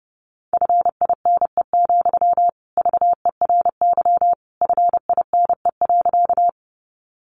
Morse code, heard as a keyed tone, about 30 wpm, 700 Hz. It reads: FINE, VERY FINE.
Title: Sentences from Top 500 Words 30wpm